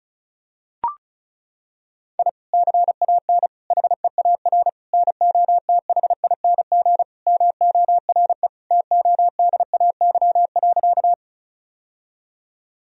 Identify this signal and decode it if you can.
Morse code, heard as a keyed tone, about 36 wpm, 700 Hz. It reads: I CAN HEAR NOTHING MORE TODAY.
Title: Sentences from Top 1000 Words - Review Part 4 - 35wpm